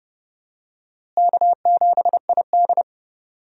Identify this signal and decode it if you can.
Morse code, heard as a keyed tone, about 30 wpm, 700 Hz. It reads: K7ID
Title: USA Call Signs - 4 Characters - Repeated 2x - 30wpm